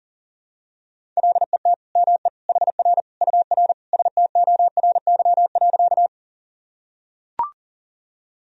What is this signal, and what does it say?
Morse code, heard as a keyed tone, about 37 wpm, 700 Hz. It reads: LET ME HR UR STORY.
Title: Sentences from Top 300 Words and Abbreviations 40wpm